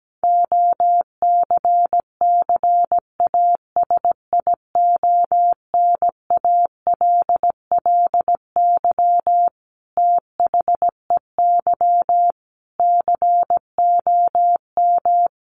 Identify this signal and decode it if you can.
Morse code, heard as a keyed tone, about 17 wpm, 700 Hz. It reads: OCCASIONALLY THEY COM
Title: Sentences from Top 1300 Words 17wpm